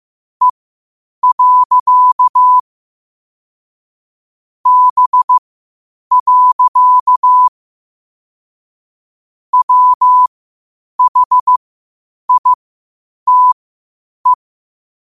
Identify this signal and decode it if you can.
Morse code, heard as a keyed tone, about 15 wpm, spaced as for 8 wpm, 1000 Hz. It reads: E. B. WHITE
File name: qotdfiles / 8wpm